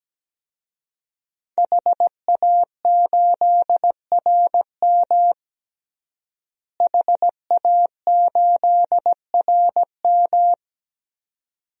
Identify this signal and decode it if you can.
Morse code, heard as a keyed tone, about 17 wpm, 700 Hz. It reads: HA8RM HA8RM
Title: DXCC-100 Call Signs - 5 Characters - Repeated 2x - 17wpm